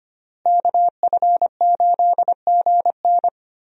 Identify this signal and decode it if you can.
Morse code, heard as a keyed tone, about 25 wpm, 700 Hz. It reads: KF8GN
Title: USA Call Signs - 5 Characters - Repeated 2x - 25wpm